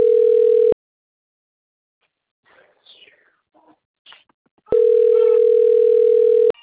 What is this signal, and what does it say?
Ringback tone (North America), 440 + 480 Hz.
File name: out- - - - - .37719